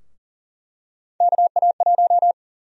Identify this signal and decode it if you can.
Morse code, heard as a keyed tone, about 39 wpm, 700 Hz. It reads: KA1
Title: USA Call Sign Prefixes - Repeated 2x - 40wpm